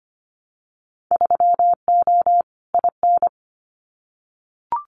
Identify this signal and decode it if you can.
Morse code, heard as a keyed tone, about 25 wpm, 700 Hz. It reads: 3O IN
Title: Winter Field Day Exchanges Encore 25wpm